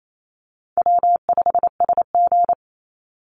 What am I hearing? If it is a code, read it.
Morse code, heard as a keyed tone, about 28 wpm, 700 Hz. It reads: W5SG
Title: Call Signs - 28wpm